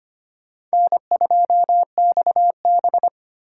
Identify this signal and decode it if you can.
Morse code, heard as a keyed tone, about 25 wpm, 700 Hz. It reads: N2XB
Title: USA Call Signs - 4 Characters - 4x Character Spacing - 25wpm